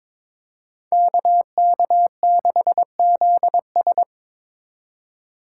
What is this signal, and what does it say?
Morse code, heard as a keyed tone, about 25 wpm, 700 Hz. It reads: KK6ZS
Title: Speed Racing - USA 2x2 Call Signs 22wpm